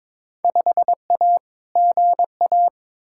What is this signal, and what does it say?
Morse code, heard as a keyed tone, about 22 wpm, 700 Hz. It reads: 5A GA
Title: Field Day Exchanges Encore - 22wpm